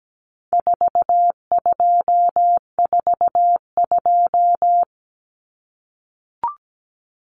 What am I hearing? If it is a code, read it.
Morse code, heard as a keyed tone, about 17 wpm, 700 Hz. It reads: 4242
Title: Numbers - 4-Digit 17wpm